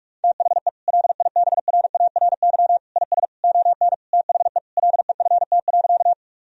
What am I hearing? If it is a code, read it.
Morse code, heard as a keyed tone, about 42 wpm, 700 Hz. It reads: THE LIBRARY IS ON THE LEFT.
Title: Sentences from Top 1000 Words - Review Part 4 - 45wpm